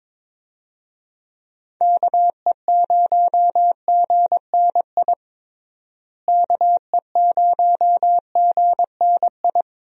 Morse code, heard as a keyed tone, about 22 wpm, 700 Hz. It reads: KE0GNI KE0GNI
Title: USA Call Signs - 6 Characters - Repeated 2x - 22wpm